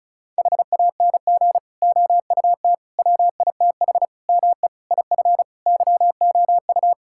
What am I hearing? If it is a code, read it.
Morse code, heard as a keyed tone, about 34 wpm, 700 Hz. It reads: HANG OUT WITH ME IF YOU
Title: Sentences from Top 1600 Words 35wpm